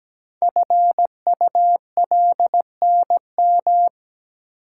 Morse code, heard as a keyed tone, about 17 wpm, 700 Hz. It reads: FULNM
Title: Code Groups - 5 Letters - 17wpm